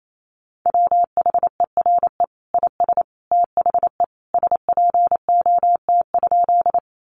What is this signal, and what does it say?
Morse code, heard as a keyed tone, about 28 wpm, 700 Hz. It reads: WHERE IS THE SPOT?
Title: Sentences from Top 1100 Words 28wpm